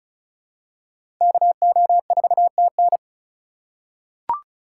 Morse code, heard as a keyed tone, about 36 wpm, 700 Hz. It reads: KO4TN
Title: USA Call Signs - 5 Characters - 3x Character Spacing - 35wpm